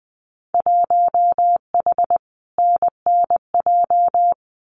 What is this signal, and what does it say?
Morse code, heard as a keyed tone, about 20 wpm, 700 Hz. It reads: 1H NNJ
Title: Winter Field Day Exchanges 20wpm